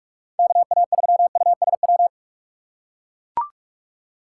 Morse code, heard as a keyed tone, about 44 wpm, 700 Hz. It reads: KA3USW
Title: USA Call Signs - 6 Characters - 6x Character Spacing - 45wpm